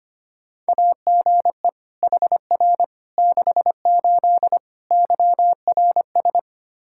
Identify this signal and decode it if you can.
Morse code, heard as a keyed tone, about 25 wpm, 700 Hz. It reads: AGE HR 68 YRS
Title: K1USN SST QSO Practice 25wpm